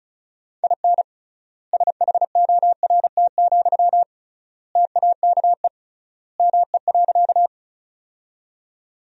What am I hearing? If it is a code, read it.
Morse code, heard as a keyed tone, about 34 wpm, 700 Hz. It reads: IN SHORT, TAKE ME.
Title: 3x Word-Spacing - Sentences from Top 400 Words - 35wpm